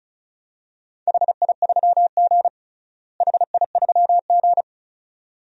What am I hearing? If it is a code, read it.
Morse code, heard as a keyed tone, about 36 wpm, 700 Hz. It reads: HI3G HI3G
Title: DXCC-100 Call Signs - 4 Characters - Repeated 2x - 35wpm